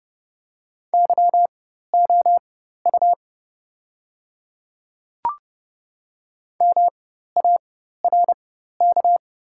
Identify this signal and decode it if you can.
Morse code, heard as a keyed tone, about 30 wpm, spaced as for 14 wpm, 700 Hz. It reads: YOU MARK
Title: 4x Character Spacing - Top 500 Words - 30wpm